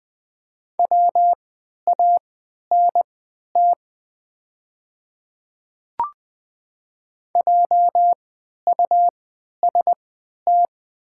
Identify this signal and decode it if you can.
Morse code, heard as a keyed tone, about 20 wpm, spaced as for 11 wpm, 700 Hz. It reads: WANT JUST
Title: 3x Character Spacing - Top 300 Words - 20wpm